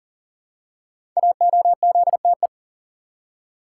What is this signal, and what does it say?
Morse code, heard as a keyed tone, about 43 wpm, 700 Hz. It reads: AOZTE
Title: Code Groups - 5 Letters - 40wpm